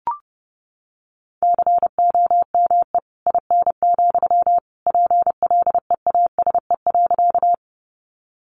Morse code, heard as a keyed tone, about 30 wpm, 700 Hz. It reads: COME IN, PLEASE.
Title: Sentences from Top 1500 Words 30wpm